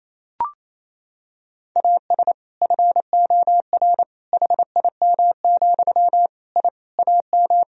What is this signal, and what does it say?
Morse code, heard as a keyed tone, about 28 wpm, 700 Hz. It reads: AS FOR HIM, I AM
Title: Sentences from Top 1000 Words - Review Part 2 - 28wpm